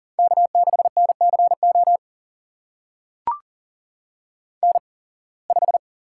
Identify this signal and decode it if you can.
Morse code, heard as a keyed tone, about 42 wpm, spaced as for 26 wpm, 700 Hz. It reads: K6NCO N5
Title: USA Call Signs - 5 Characters - 8x Character Spacing - 40wpm